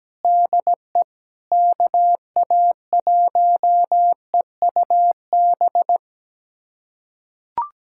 Morse code, heard as a keyed tone, about 17 wpm, 700 Hz. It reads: DE KA1EUB
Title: USA Callsign DE Callsign 17wpm